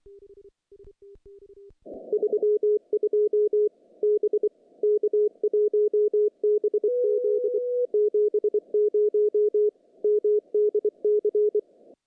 Morse code, heard as a keyed tone, about 24 wpm, 410 Hz. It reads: 32 B K1BZ 70 MDC
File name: contests / CW-SS-k5tr- -0000